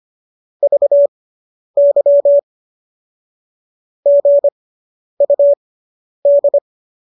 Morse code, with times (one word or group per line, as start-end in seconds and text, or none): 0.6-2.4 VY
4.1-6.6 GUD